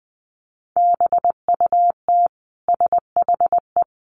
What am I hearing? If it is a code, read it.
Morse code, heard as a keyed tone, about 20 wpm, 700 Hz. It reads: BUT SHE
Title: Sentences from Top 700 Words 20wpm